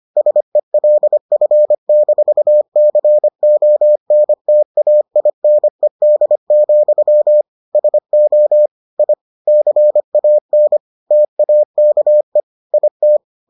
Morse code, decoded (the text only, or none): SELF-CONTAINED, SO I CAN TAKE IT